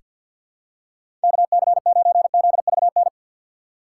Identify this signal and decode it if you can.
Morse code, heard as a keyed tone, about 47 wpm, 700 Hz. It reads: KC9ZVN
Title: USA Call Signs - 6 Characters - 6x Character Spacing - 50wpm